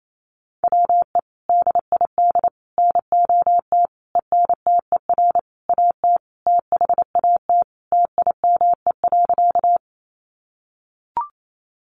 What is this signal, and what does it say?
Morse code, heard as a keyed tone, about 28 wpm, 700 Hz. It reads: WE DID NOT ENTER AT THAT TIME.
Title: Sentences from Top 1300 Words 28wpm